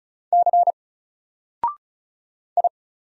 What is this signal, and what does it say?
Morse code, heard as a keyed tone, about 36 wpm, 700 Hz. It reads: C I
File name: Single-Letter-Number-35wpm